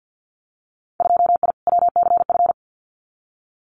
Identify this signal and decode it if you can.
Morse code, heard as a keyed tone, about 50 wpm, 700 Hz. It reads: 2I WCF
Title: Winter Field Day Exchanges 50wpm